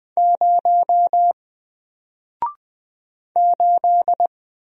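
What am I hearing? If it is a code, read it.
Morse code, heard as a keyed tone, about 20 wpm, 700 Hz. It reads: 0 8
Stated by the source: Single letter and number Morse drill at 20 wpm, from N4LCD.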